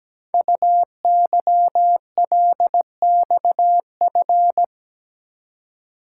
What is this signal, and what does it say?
Morse code, heard as a keyed tone, about 17 wpm, 700 Hz. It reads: UYLXF
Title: Code Groups - 5 Letters - 17wpm